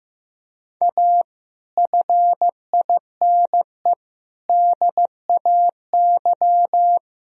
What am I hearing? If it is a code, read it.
Morse code, heard as a keyed tone, about 14 wpm, 700 Hz. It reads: A FINE DAY
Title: Sentences from Top 1000 Words - Review Part 4 - 15wpm